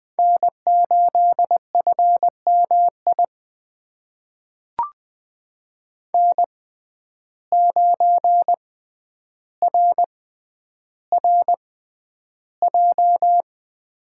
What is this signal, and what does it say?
Morse code, heard as a keyed tone, about 20 wpm, spaced as for 10 wpm, 700 Hz. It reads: N8FMI N9RRJ
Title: USA Call Signs - 5 Characters - 6x Character Spacing - 20wpm